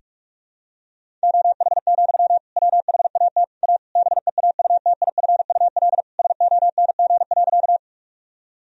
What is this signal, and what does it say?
Morse code, heard as a keyed tone, about 41 wpm, 700 Hz. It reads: OH, WHAT A BEAUTIFUL SONG.
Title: Sentences from Top 1000 Words - Review Part 2 - 45wpm